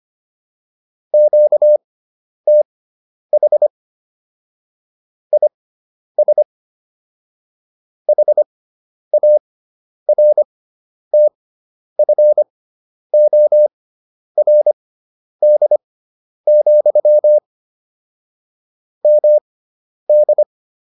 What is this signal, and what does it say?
Morse code, heard as a keyed tone, about 25 wpm, spaced as for 10 wpm, 600 Hz. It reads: QTH IS HARTFORD, MD
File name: QSO401 10